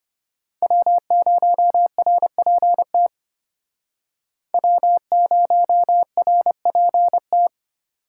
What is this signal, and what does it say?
Morse code, heard as a keyed tone, about 32 wpm, 700 Hz. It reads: W0RPT W0RPT